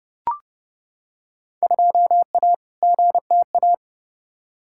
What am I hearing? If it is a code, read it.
Morse code, heard as a keyed tone, about 30 wpm, 700 Hz. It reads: 2A GTA
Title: Field Day Exchanges Encore - 30wpm